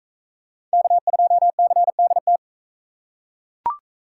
Morse code, heard as a keyed tone, about 41 wpm, 700 Hz. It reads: K2CDT